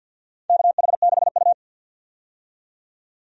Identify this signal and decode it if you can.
Morse code, heard as a keyed tone, about 52 wpm, 700 Hz. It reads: KH6U
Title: DXCC-100 Call Signs - 4 Characters - 50wpm